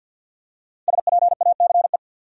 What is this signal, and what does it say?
Morse code, heard as a keyed tone, about 52 wpm, 700 Hz. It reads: SPACE
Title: Five Letter Words - 50wpm